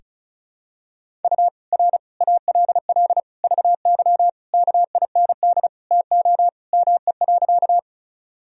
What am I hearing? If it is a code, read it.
Morse code, heard as a keyed tone, about 36 wpm, 700 Hz. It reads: U R ALL VY KIND TO ME.
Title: Sentences from Top 200 Words and Abbreviations 35wpm